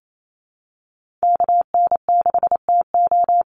Morse code, heard as a keyed tone, about 28 wpm, 700 Hz. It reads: KN6TO
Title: USA Call Signs 28wpm